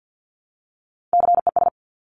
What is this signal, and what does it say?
Morse code, heard as a keyed tone, about 50 wpm, 700 Hz. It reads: CES
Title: Top Three Letter Combinations in Words 50wpm